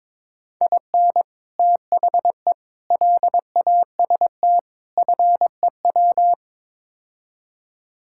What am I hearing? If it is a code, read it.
Morse code, heard as a keyed tone, about 21 wpm, 700 Hz. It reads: IN THE LAST FEW
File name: practice / 22wpm / Build A Phrase - 5-Word Phrases - 22wpm